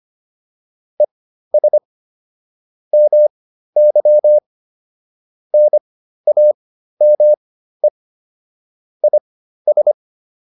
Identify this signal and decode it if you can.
Morse code, heard as a keyed tone, about 25 wpm, spaced as for 13 wpm, 600 Hz. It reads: ES MY NAME IS